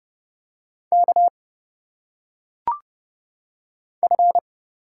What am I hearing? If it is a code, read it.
Morse code, heard as a keyed tone, about 30 wpm, 700 Hz. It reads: K F
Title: Single Letters - Phonetic Pronunciation 30wpm